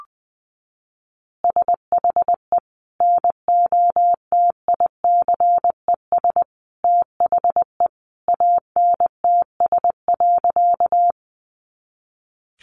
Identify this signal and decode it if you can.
Morse code, heard as a keyed tone, about 20 wpm, 700 Hz. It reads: SHE NOTICES THE ANTS.
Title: Easy Phrases 20wpm